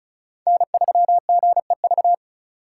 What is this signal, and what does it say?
Morse code, heard as a keyed tone, about 36 wpm, 700 Hz. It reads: N3GEV
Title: Code Groups - 5 Letter:Numbers - 35wpm